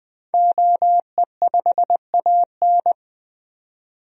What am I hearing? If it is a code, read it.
Morse code, heard as a keyed tone, about 20 wpm, 700 Hz. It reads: OE5AN
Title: DXCC-100 Call Signs - 5 Characters - 20wpm